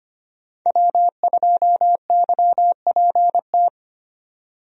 Morse code, heard as a keyed tone, about 25 wpm, 690 Hz. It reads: W2YPT
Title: USA Call Signs - 5 Characters - 25wpm